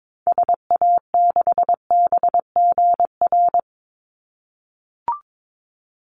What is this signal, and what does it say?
Morse code, heard as a keyed tone, about 22 wpm, 700 Hz. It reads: SA6BGR
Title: Call Signs - 22wpm